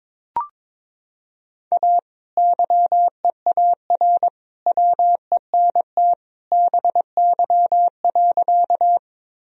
Morse code, heard as a keyed tone, about 22 wpm, 700 Hz. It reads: A YEAR WENT BY.
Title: Sentences from Top 1000 Words - Review Part 1 - 22wpm